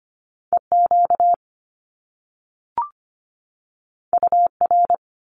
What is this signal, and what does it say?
Morse code, heard as a keyed tone, about 25 wpm, 700 Hz. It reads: EQ UR
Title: Top Two Letter Combinations in Words 25wpm